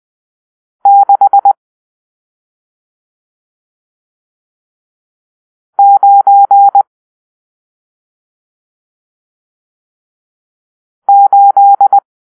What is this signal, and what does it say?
Morse code, heard as a keyed tone, about 20 wpm, 800 Hz. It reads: 6 9 8
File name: arrl / Grp-7 Drill-1